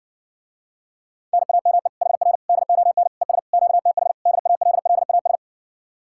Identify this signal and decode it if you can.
Morse code, heard as a keyed tone, about 71 wpm, 700 Hz. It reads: NAME HR DON ES QTH DALLAS